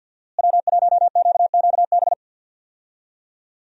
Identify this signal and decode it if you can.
Morse code, heard as a keyed tone, about 53 wpm, 700 Hz. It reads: W1QQB
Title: USA Call Signs - 5 Characters - Repeated 2x - 50wpm